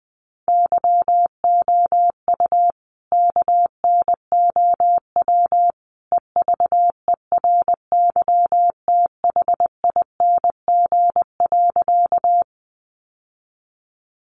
Morse code, heard as a keyed tone, about 19 wpm, 700 Hz. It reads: YOU KNOW EVERYTHING.
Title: Sentences from Top 700 Words 20wpm